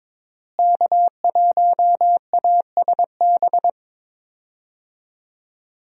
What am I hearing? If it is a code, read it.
Morse code, heard as a keyed tone, about 22 wpm, 700 Hz. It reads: K1ASB
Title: USA Call Signs - 5 Characters - Repeated 3x - 22wpm